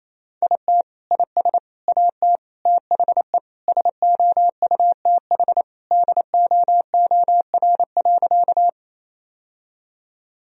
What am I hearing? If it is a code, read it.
Morse code, heard as a keyed tone, about 27 wpm, 700 Hz. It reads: IT IS AT THE SOUTH DOOR.
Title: Sentences from Top 1000 Words - Review Part 2 - 28wpm